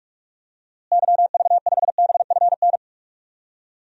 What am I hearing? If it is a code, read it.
Morse code, heard as a keyed tone, about 43 wpm, 700 Hz. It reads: YV5BFN
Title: DXCC-100 Call Signs - 6 Characters - Repeated 2x - 45wpm